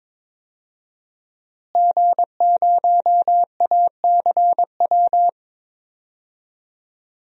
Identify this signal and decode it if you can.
Morse code, heard as a keyed tone, about 22 wpm, 700 Hz. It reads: G0ACW
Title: DXCC-100 Call Signs - 5 Characters - Repeated 3x - 22wpm